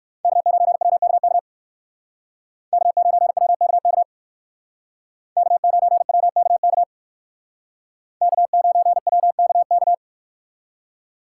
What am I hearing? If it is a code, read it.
Morse code, heard as a keyed tone, about 53 wpm, 700 Hz. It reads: K9WKK K9WKK K9WKK K9WKK
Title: Speed Racing - USA Call Signs 45wpm